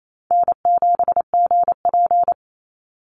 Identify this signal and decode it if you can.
Morse code, heard as a keyed tone, about 28 wpm, 700 Hz. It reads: N7GP
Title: USA Call Signs 28wpm